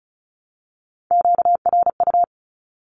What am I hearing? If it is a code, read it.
Morse code, heard as a keyed tone, about 34 wpm, 700 Hz. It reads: QRU
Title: Q-Codes Repeated 3x 35wpm